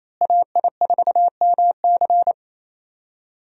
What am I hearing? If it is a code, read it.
Morse code, heard as a keyed tone, about 28 wpm, 700 Hz. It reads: AI4MC
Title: USA Call Signs - 5 Characters - Repeated 3x - 28wpm